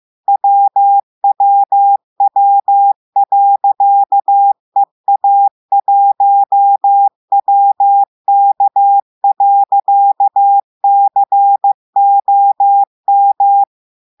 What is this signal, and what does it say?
Morse code, heard as a keyed tone, about 15 wpm, 800 Hz. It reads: WWW.EA1WK.COM
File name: eh1ard-amateur-radio-day-2016 mf feed 1